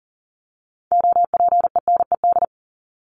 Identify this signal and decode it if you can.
Morse code, heard as a keyed tone, about 40 wpm, 700 Hz. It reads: OPENED
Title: ED Suffix - Top 100 Words with Base Words 40wpm